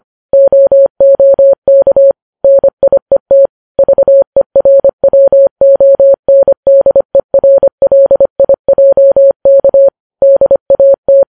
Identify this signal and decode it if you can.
Morse code, heard as a keyed tone, about 25 wpm, 550 Hz. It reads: OOK NIET VERWONDERLIJK DAT
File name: Files Web tls / Morse on9fv / morse25kps